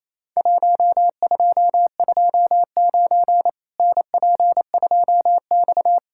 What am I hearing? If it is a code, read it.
Morse code, heard as a keyed tone, about 28 wpm, 700 Hz. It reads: 1229 NP2X
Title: Worked All Europe DX Contest - QTC Exchange - 28wpm